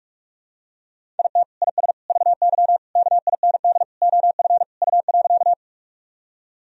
Morse code, heard as a keyed tone, about 46 wpm, 700 Hz. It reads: IT IS VY KIND OF U.
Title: Sentences from Top 200 Words and Abbreviations 45wpm